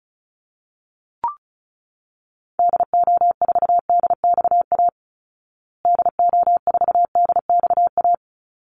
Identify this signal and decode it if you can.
Morse code, heard as a keyed tone, about 34 wpm, 700 Hz. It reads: DO4DXA DO4DXA
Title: Call Signs Repeated 3x - 35wpm